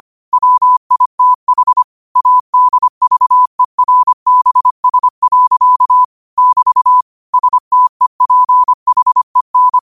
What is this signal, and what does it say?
Morse code, heard as a keyed tone, about 25 wpm, 1000 Hz. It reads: WITH ADVERBS. <BT> STEPHEN
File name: qotdfiles / 25wpm